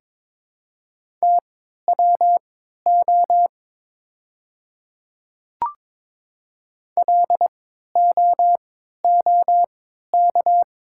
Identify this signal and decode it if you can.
Morse code, heard as a keyed tone, about 22 wpm, spaced as for 12 wpm, 700 Hz. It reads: TWO LOOK
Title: 3x Character Spacing - Top 200 Words - 22wpm